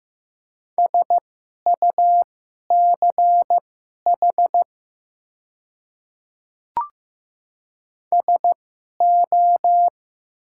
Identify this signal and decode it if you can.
Morse code, heard as a keyed tone, about 15 wpm, spaced as for 11 wpm, 700 Hz. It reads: SUCH SO
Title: 2x Character Spacing - Top 200 Words - 15wpm